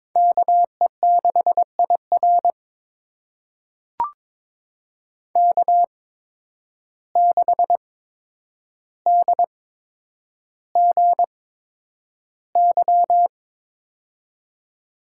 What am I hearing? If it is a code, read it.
Morse code, heard as a keyed tone, about 22 wpm, spaced as for 9 wpm, 700 Hz. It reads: KE6IR K6DGY
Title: USA Call Signs - 5 Characters - 8x Character Spacing - 22wpm